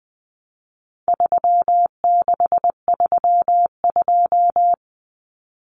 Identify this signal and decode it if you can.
Morse code, heard as a keyed tone, about 20 wpm, 700 Hz. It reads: 3632
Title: Common Cut Numbers - 4-Digit 20wpm